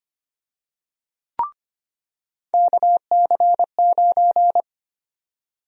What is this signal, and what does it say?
Morse code, heard as a keyed tone, about 25 wpm, 700 Hz. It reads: KC9
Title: USA Call Sign Prefixes - 25wpm